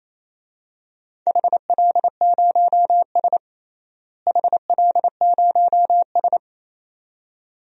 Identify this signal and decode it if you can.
Morse code, heard as a keyed tone, about 28 wpm, 700 Hz. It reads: HL0S HL0S
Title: DXCC-100 Call Signs - 4 Characters - Repeated 2x - 28wpm